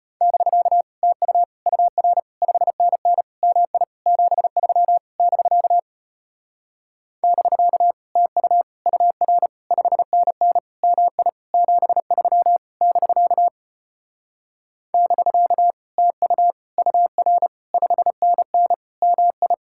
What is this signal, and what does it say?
Morse code, heard as a keyed tone, about 33 wpm, 700 Hz. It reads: <BK> TU UR 5NN MI 73 <BK> <BK> TU UR 5NN MI 73 <BK> <BK> TU UR 5NN MI